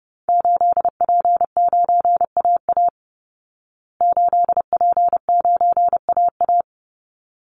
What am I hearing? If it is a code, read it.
Morse code, heard as a keyed tone, about 30 wpm, 700 Hz. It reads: 8P9AA 8P9AA
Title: Call Signs Repeated 3x - 30wpm